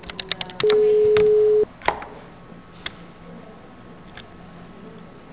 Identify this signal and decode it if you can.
Telephone call progress tone (Europe, 425 Hz), heard as a telephone network's tone, about 420 Hz. Ringback tone.